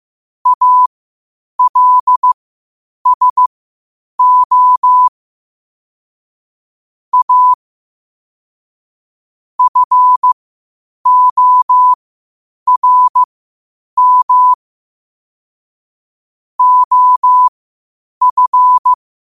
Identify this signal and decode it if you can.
Morse code, heard as a keyed tone, about 15 wpm, spaced as for 8 wpm, 1000 Hz. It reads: ALSO A FORM OF
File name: qotdfiles / 8wpm